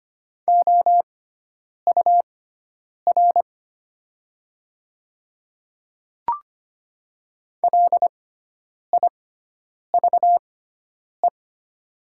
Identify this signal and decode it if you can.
Morse code, heard as a keyed tone, about 25 wpm, spaced as for 9 wpm, 690 Hz. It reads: OUR LIVE
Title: 6x Character Spacing - Top 200 Words - 25wpm